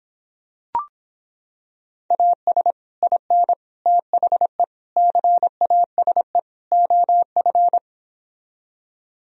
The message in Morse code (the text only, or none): AS IN THE CASE OF